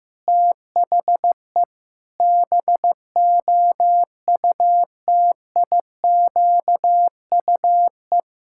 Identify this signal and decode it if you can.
Morse code, heard as a keyed tone, about 16 wpm, 700 Hz. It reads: THE BOUTIQUE
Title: Pangram Challenge v2 - 15wpm